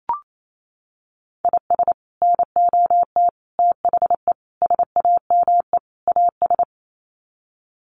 Morse code, heard as a keyed tone, about 28 wpm, 700 Hz. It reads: IS NOT THE SAME AS
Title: Sets of 5 Words Encore 28wpm